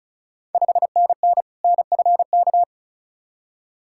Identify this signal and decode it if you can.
Morse code, heard as a keyed tone, about 36 wpm, 690 Hz. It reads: 5NN NFK
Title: Ontario QSO Party Exchange 35wpm